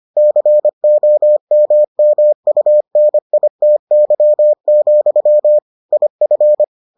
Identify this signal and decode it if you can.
Morse code, heard as a keyed tone, about 25 wpm, 600 Hz. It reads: COMMUNITY, IF